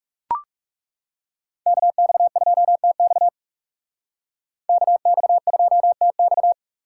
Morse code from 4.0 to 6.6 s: KX2TX

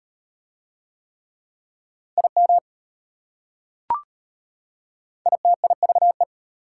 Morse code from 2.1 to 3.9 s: IM

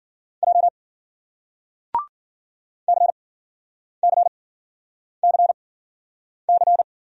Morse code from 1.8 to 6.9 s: CCCC

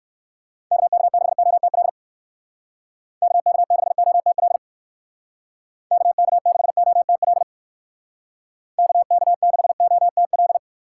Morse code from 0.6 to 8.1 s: KK6OTL KK6OTL KK6OTL

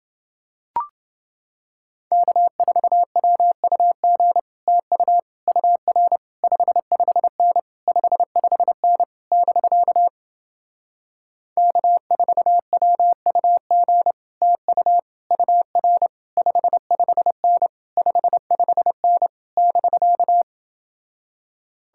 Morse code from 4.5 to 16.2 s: TU UR 55N 55N <BK> K4WUG TU UR

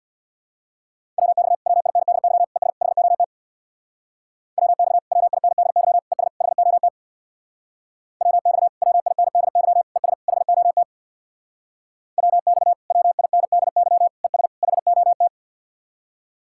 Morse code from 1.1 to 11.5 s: WX WINDY ES HOT WX WINDY ES HOT WX WINDY ES HOT

